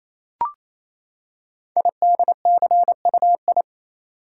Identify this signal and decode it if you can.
Morse code, heard as a keyed tone, about 27 wpm, 700 Hz. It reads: IDCUI
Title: Code Groups - 5 Letter:Numbers - 28wpm